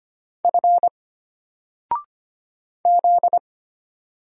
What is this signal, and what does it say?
Morse code, heard as a keyed tone, about 25 wpm, 700 Hz. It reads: F Z
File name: Single-Letter-Number-25wpm